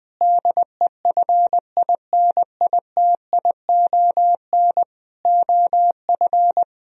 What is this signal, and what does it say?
Morse code, heard as a keyed tone, about 20 wpm, 700 Hz. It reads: DEFINITION OF
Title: Sentences from Top 1000 Words - Review Part 4 - 20wpm